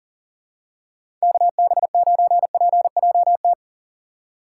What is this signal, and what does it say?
Morse code, heard as a keyed tone, about 40 wpm, 700 Hz. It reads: KB9PJT
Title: USA Call Signs - 6x Character Spacing - 40wpm